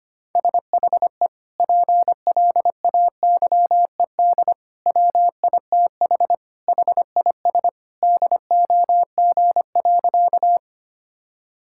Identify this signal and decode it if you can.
Morse code, heard as a keyed tone, about 25 wpm, 700 Hz. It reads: SHE PLAYED WITH HIS DOG.
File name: practice / 25wpm / Sentences with 2-Word Phrases - 25wpm